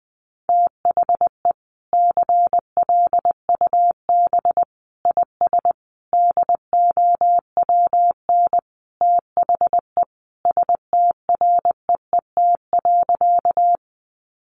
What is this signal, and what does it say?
Morse code, heard as a keyed tone, about 20 wpm, 700 Hz. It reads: THE CLUB IS DOWN THE STREET.
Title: Sentences from Top 1200 Words 20wpm